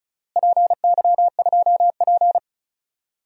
Morse code, heard as a keyed tone, about 36 wpm, 700 Hz. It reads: PY2P
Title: DXCC-100 Call Signs - 4 Characters - Repeated 3x - 35wpm